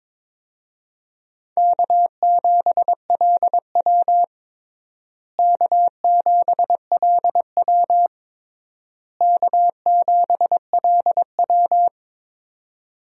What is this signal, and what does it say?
Morse code, heard as a keyed tone, about 22 wpm, 700 Hz. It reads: K7LW K7LW K7LW
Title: USA Call Signs - 4 Characters - Repeated 3x - 22wpm